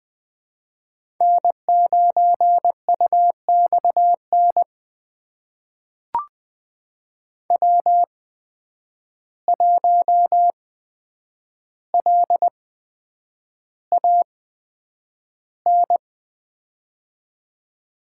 Morse code, heard as a keyed tone, about 20 wpm, spaced as for 9 wpm, 700 Hz. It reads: N9UXN W1LAN
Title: USA Call Signs - 5 Characters - 8x Character Spacing - 20wpm